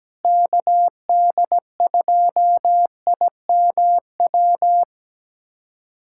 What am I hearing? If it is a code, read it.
Morse code, heard as a keyed tone, about 17 wpm, 700 Hz. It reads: KD2IMW
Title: USA Callsign DE Callsign 17wpm